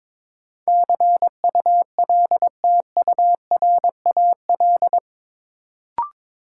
Morse code, heard as a keyed tone, about 22 wpm, 700 Hz. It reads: CULTURAL
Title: Sentences from Top 1500 Words - New Word Drill - 22wpm